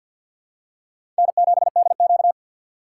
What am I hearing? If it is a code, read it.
Morse code, heard as a keyed tone, about 48 wpm, 700 Hz. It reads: N7DQ
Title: USA Call Signs - 4 Characters - 6x Character Spacing - 50wpm